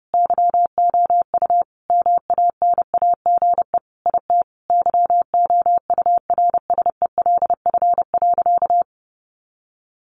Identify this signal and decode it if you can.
Morse code, heard as a keyed tone, about 30 wpm, 700 Hz. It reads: YOU MANAGE IT YOURSELF.
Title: Sentences from Top 1400 Words 30wpm